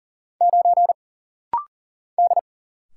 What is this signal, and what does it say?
Morse code, heard as a keyed tone, about 40 wpm, 700 Hz. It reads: 9 D
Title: Single Letter-Number - Rapid-Fire 40wpm